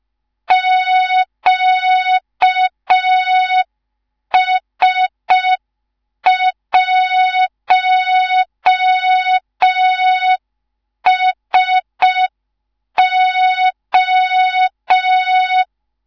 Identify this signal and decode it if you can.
Morse code, heard as a keyed tone, about 5 wpm, 3700 Hz. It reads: QS1SO
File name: Morze-HA2MN / hang / 19 GYAKORLAT